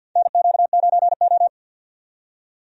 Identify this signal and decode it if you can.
Morse code, heard as a keyed tone, about 51 wpm, 700 Hz. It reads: NQ9O
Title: USA Call Signs - 4 Characters - 50wpm